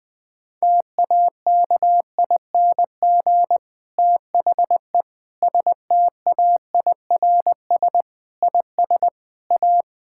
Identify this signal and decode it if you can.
Morse code, heard as a keyed tone, about 20 wpm, 700 Hz. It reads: TAKING THE STAIRS IS A
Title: Difficult Sentences with Hints - 20wpm